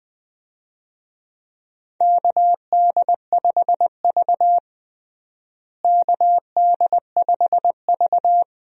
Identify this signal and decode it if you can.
Morse code, heard as a keyed tone, about 21 wpm, 700 Hz. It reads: KD5V KD5V
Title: USA Call Signs - 4 Characters - Repeated 3x - 20wpm